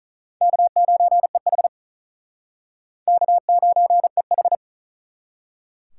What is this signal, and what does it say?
Morse code, heard as a keyed tone, about 48 wpm, 700 Hz. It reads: K9EH K9EH